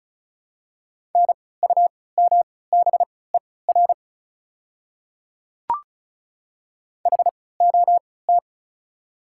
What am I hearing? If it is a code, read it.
Morse code, heard as a keyed tone, about 36 wpm, spaced as for 20 wpm, 690 Hz. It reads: NUMBER HOT
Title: 3x Character Spacing - Top 100 Words - 35wpm